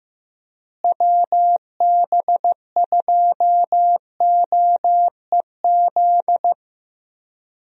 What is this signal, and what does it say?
Morse code, heard as a keyed tone, about 15 wpm, 700 Hz. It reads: WB2OEZ
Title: USA Call Signs - 6 Characters - Repeated 2x - 15wpm